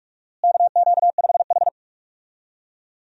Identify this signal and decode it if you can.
Morse code, heard as a keyed tone, about 46 wpm, 700 Hz. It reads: KQ5H
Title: USA Call Signs - 4 Characters - Repeated 3x - 45wpm